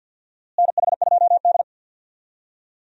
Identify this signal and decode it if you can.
Morse code, heard as a keyed tone, about 47 wpm, 700 Hz. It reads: NH2D
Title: USA Call Signs - 4 Characters - Repeated 3x - 50wpm